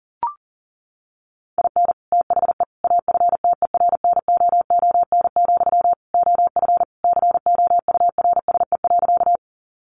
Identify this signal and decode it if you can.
Morse code, heard as a keyed tone, about 40 wpm, 700 Hz. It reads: IN THE AFTERNOON, OF COURSE.
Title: Sentences from Top 1500 Words 40wpm